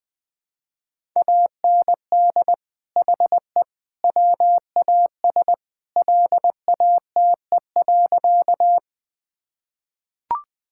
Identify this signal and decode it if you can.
Morse code, heard as a keyed tone, about 20 wpm, 700 Hz. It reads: AND HE WAS LATE.
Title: Sentences with 3-Word Phrases - Spoken Beginning - 20wpm